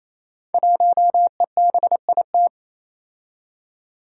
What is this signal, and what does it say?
Morse code, heard as a keyed tone, about 27 wpm, 700 Hz. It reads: 1EBIT
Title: Five Challenging Letter-Number Combinations 28wpm